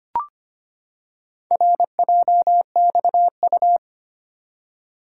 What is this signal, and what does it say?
Morse code, heard as a keyed tone, about 25 wpm, 690 Hz. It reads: RJXU
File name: practice / 25wpm / Four Letter-Number Combinations - 25wpm